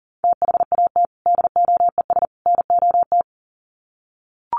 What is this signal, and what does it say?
Morse code, heard as a keyed tone, about 40 wpm, 700 Hz. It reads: THAT DOES NOT
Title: Sets of 3 Words 40wpm